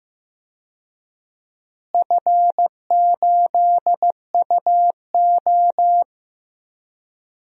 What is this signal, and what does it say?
Morse code, heard as a keyed tone, about 15 wpm, 700 Hz. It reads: F8UO
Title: DXCC-100 Call Signs - 4 Characters - Repeated 2x - 15wpm